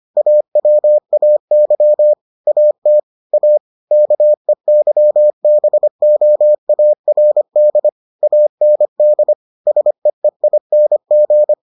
Morse code, heard as a keyed tone, about 25 wpm, 600 Hz. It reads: AWAY AT A KEYBOARD AND SEEING